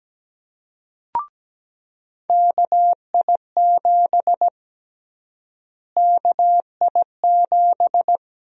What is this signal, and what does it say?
Morse code, heard as a keyed tone, about 17 wpm, 700 Hz. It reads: KI7 KI7
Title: USA Call Sign Prefixes - Repeated 2x - 17wpm